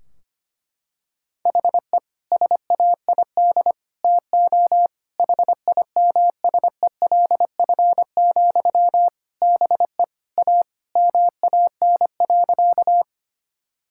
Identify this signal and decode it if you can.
Morse code, heard as a keyed tone, about 25 wpm, 700 Hz. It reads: HE SAID TO HIMSELF, BE A MAN.
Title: Sentences from Top 1600 Words 25wpm